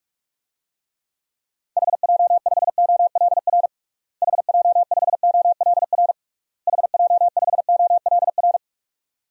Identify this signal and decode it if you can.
Morse code, heard as a keyed tone, about 41 wpm, 700 Hz. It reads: HJ5OLR HJ5OLR HJ5OLR
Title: DXCC-100 Call Signs - 6 Characters - Repeated 3x - 45wpm